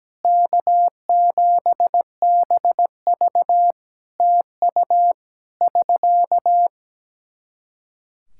Morse code, heard as a keyed tone, about 17 wpm, 700 Hz. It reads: K7BV TU <SK>
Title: Practice QSO - Exchange 4 - Ending - 17wpm